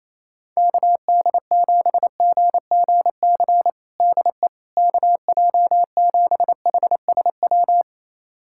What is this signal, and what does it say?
Morse code, heard as a keyed tone, about 27 wpm, 700 Hz. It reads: KD7GGC DE KJ7HSW
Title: USA Callsign DE Callsign - 6 Characters - 28wpm